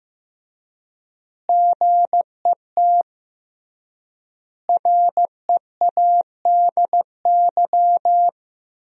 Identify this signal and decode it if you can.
Morse code, heard as a keyed tone, about 15 wpm, 700 Hz. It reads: GET READY
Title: 3x Word-Spacing - Sentences from Top 400 Words - 15wpm